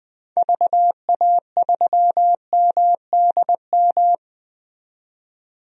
Morse code, heard as a keyed tone, about 21 wpm, 700 Hz. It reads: VA3MDM
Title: DXCC-100 Call Signs - 6 Characters - 20wpm